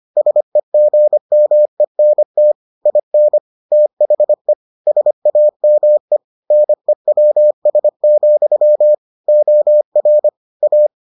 Morse code, heard as a keyed tone, about 25 wpm, 600 Hz. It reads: SEGMENT IN THE SAME NEWS, OR A